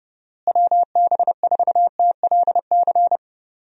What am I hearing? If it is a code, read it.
Morse code, heard as a keyed tone, about 32 wpm, 700 Hz. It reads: WB4TLC
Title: USA Call Signs - 6 Characters - Repeated 2x - 30wpm